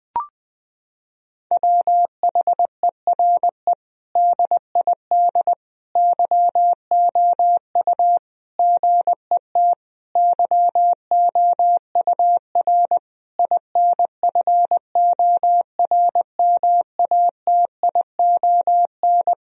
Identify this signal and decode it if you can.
Morse code, heard as a keyed tone, about 21 wpm, 700 Hz. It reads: WHERE DID YOU GET YOUR INFORMATION
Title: Sentences from Top 1000 Words - Review Part 3 - 20wpm